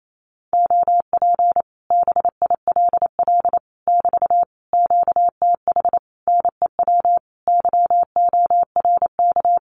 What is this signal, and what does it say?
Morse code, heard as a keyed tone, about 28 wpm, 700 Hz. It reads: OP BILL <BT> QTH NEW YORK
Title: Common QSO Phrases Encore 28wpm